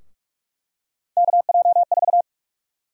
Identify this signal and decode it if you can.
Morse code, heard as a keyed tone, about 46 wpm, 700 Hz. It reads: KJ4
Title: USA Call Sign Prefixes - 45wpm